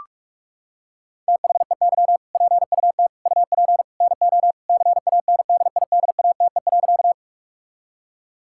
Morse code, heard as a keyed tone, about 43 wpm, 700 Hz. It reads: THEY PUT UP NO CANDIDATE.
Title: Sentences from Top 1600 Words 45wpm